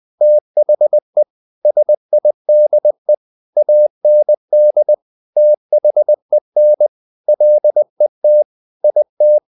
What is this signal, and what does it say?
Morse code, heard as a keyed tone, about 20 wpm, 600 Hz. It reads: THE SIDE AND THEN LET IT
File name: the old man and the sea part 2 20wpm